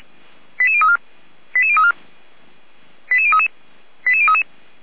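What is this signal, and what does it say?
Five-tone selective call (ZVEI), heard as quick runs of five tones, tones of about 70 ms. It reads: 88034 88034 88033 88033
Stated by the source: ZVEI five-tone selective call received by SDR, address 89084.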